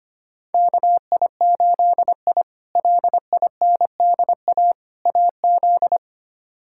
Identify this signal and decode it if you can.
Morse code, heard as a keyed tone, about 25 wpm, 700 Hz. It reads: KI8I LINDA AZ
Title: K1USN SST QSO Practice - 25wpm